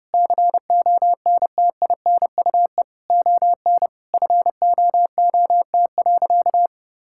Morse code, heard as a keyed tone, about 30 wpm, 700 Hz. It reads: CONTINUE ON FOOT.
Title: Sentences from Top 1000 Words - Review Part 6 - 30wpm